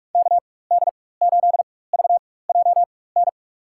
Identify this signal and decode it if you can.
Morse code, heard as a keyed tone, about 46 wpm, spaced as for 21 wpm, 700 Hz. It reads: KD8VJN
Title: USA Call Signs - 6 Characters - 4x Character Spacing - 45wpm